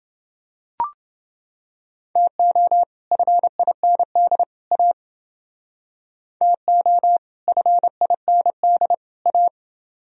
Morse code, heard as a keyed tone, about 30 wpm, 700 Hz. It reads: TO FIND A TO FIND A